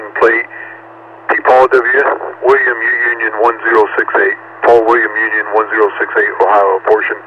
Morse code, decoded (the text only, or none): none